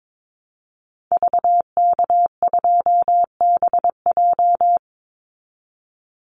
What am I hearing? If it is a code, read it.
Morse code, heard as a keyed tone, about 22 wpm, 700 Hz. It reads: VK2BJ
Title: Call Signs Repeated 2x - 22wpm